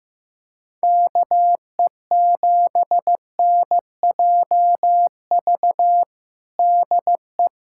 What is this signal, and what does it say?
Morse code, heard as a keyed tone, about 15 wpm, 700 Hz. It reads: KE7NJV DE
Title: USA Callsign DE Callsign 15wpm